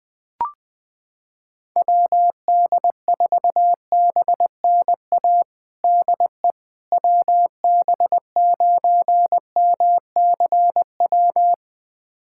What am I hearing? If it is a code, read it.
Morse code, heard as a keyed tone, about 20 wpm, 700 Hz. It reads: WD4BNA DE WB9MCW
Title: USA Callsign DE Callsign - 6 Characters - 20wpm